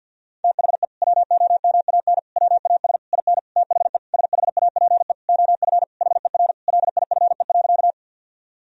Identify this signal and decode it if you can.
Morse code, heard as a keyed tone, about 50 wpm, 700 Hz. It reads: THE WOMAN WAS IN THE SHAPE OF HER LIFE.
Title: Sentences from Top 1000 Words - Review Part 3 - 50wpm